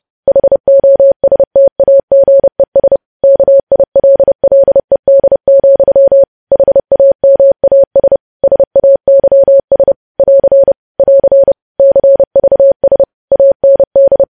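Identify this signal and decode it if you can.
Morse code, heard as a keyed tone, about 30 wpm, 550 Hz. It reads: HOSTAGES KILLED, HAMAS SAYS <AR> <AR> CVS AND
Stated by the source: Morse code news headlines sent at 30 words per minute.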